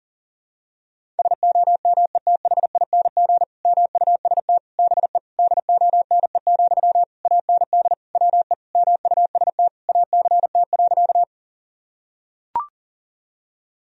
Morse code, heard as a keyed tone, about 40 wpm, 700 Hz. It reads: SOMETHING MUST BE DONE, AND WE MUST ACT.
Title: Sentences from Top 1000 Words - Review Part 1 - 40wpm